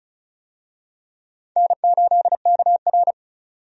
Morse code, heard as a keyed tone, about 36 wpm, 700 Hz. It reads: N8KR
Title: USA Call Signs - 4 Characters - 35wpm